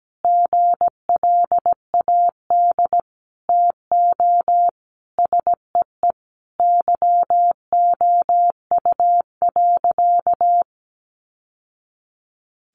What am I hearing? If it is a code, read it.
Morse code, heard as a keyed tone, about 17 wpm, 700 Hz. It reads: GLAD TO SEE YOU.
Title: Sentences from Top 800 Words 17wpm